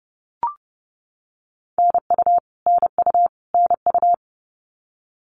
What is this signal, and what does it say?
Morse code, heard as a keyed tone, about 29 wpm, 700 Hz. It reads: NU NU NU
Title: Canadian Provinces and Territories - Repeated 3x - 30wpm